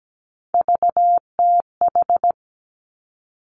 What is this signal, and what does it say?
Morse code, heard as a keyed tone, about 17 wpm, 700 Hz. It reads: VTH
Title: Three Challenging Letter-Number Combinations 17wpm